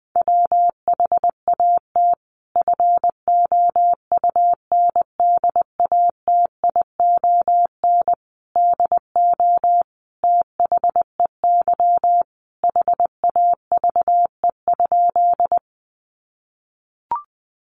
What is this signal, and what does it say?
Morse code, heard as a keyed tone, about 19 wpm, 700 Hz. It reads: WHAT FOUNDATION DO THEY HAVE?
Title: Sentences from Top 1200 Words 20wpm